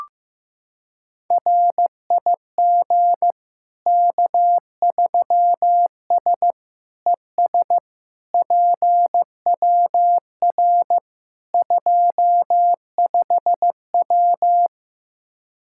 Morse code, heard as a keyed tone, about 15 wpm, 700 Hz. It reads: RIG K3S ES PWR 25W
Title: Practice QSO - Exchange 2 - Info - 15wpm